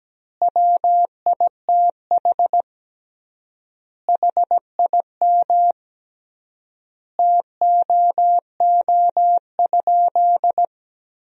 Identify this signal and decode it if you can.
Morse code, heard as a keyed tone, about 17 wpm, 700 Hz. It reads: WITH HIM TOO?
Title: 3x Word-Spacing - Sentences from Top 500 Words - 17wpm